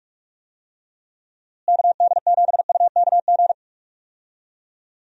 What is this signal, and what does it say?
Morse code, heard as a keyed tone, about 46 wpm, 700 Hz. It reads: KD7UKG
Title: USA Call Signs - 6 Characters - Repeated 2x - 45wpm